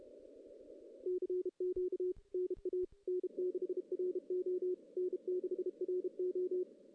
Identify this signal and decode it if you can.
Morse code, heard as a keyed tone, about 35 wpm, 370 Hz. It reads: CQ NA N6RO N6RO
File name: 2016 JAN CW NAQP / 2016 Jan naqp k5ot - 0-6